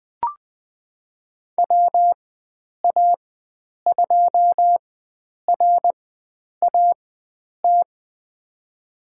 Morse code, heard as a keyed tone, about 20 wpm, spaced as for 9 wpm, 700 Hz. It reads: WA2RAT
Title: USA Call Signs - 6 Characters - 4x Character Spacing - 20wpm